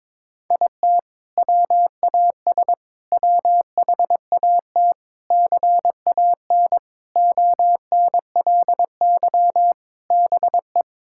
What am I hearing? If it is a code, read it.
Morse code, heard as a keyed tone, about 22 wpm, 700 Hz. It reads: IT WAS WHAT CAN ONLY BE
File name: Sentences with 7-Word Phrases - 22wpm